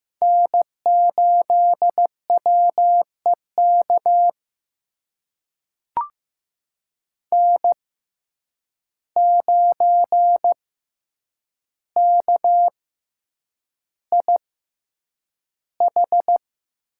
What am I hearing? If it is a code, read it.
Morse code, heard as a keyed tone, about 16 wpm, spaced as for 8 wpm, 700 Hz. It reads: N8WEK N9KIH
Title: USA Call Signs - 5 Characters - 6x Character Spacing - 15wpm